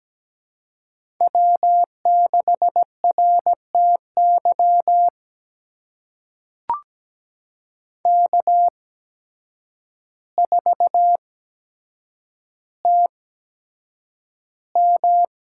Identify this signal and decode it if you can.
Morse code, heard as a keyed tone, about 17 wpm, spaced as for 8 wpm, 700 Hz. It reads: W6RTY K4TM
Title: USA Call Signs - 5 Characters - 8x Character Spacing - 17wpm